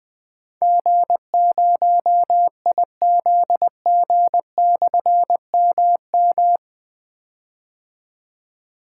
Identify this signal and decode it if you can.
Morse code, heard as a keyed tone, about 20 wpm, 700 Hz. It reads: G0IZG/MM
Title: Difficult Call Signs 20wpm